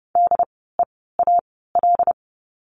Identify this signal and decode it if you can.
Morse code, heard as a keyed tone, about 30 wpm, spaced as for 17 wpm, 700 Hz. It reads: DEAL
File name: Lesson 16f - Letter c - Words 30wpm